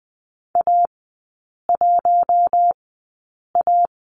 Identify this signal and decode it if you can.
Morse code, heard as a keyed tone, about 20 wpm, spaced as for 8 wpm, 700 Hz. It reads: A1A
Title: 047 Lesson 9e - Number 4 - Three Characters 20wpm